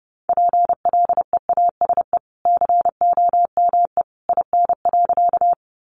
Morse code, heard as a keyed tone, about 28 wpm, 700 Hz. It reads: PLEASE COME IN.
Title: Sentences from Top 1500 Words 30wpm